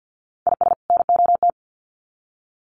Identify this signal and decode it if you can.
Morse code, heard as a keyed tone, about 50 wpm, 700 Hz. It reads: IS NOT
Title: Sets of 2 Words 50wpm